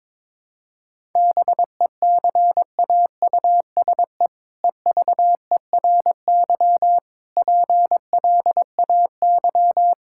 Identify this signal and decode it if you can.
Morse code, heard as a keyed tone, about 22 wpm, 700 Hz. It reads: BECAUSE EVERY PLAY
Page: morse-code-ninja-practice-22wpm